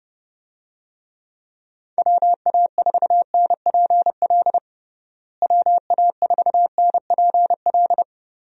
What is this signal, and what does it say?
Morse code, heard as a keyed tone, about 30 wpm, 700 Hz. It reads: WA4NPL WA4NPL
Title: USA Call Signs - 6 Characters - Repeated 3x - 30wpm